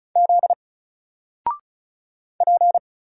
Morse code, heard as a keyed tone, about 36 wpm, 700 Hz. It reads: Z P
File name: Single-Letter-Number-35wpm